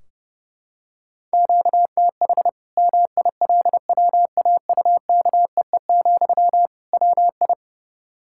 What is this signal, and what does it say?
Morse code, heard as a keyed tone, about 30 wpm, 700 Hz. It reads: QTH MILWAUKEE, WI
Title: Practice QSO - Exchange 1 - Beginning - 30wpm